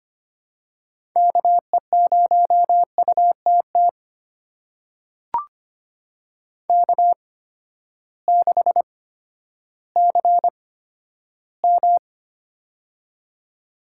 Morse code, heard as a keyed tone, about 25 wpm, spaced as for 12 wpm, 700 Hz. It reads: KE0UTT K6CM
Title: USA Call Signs - 8x Character Spacing - 25wpm